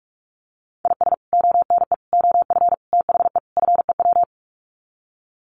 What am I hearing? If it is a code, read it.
Morse code, heard as a keyed tone, about 44 wpm, 700 Hz. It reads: IS ONE OF THE FEW
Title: Sets of 5 Words 45wpm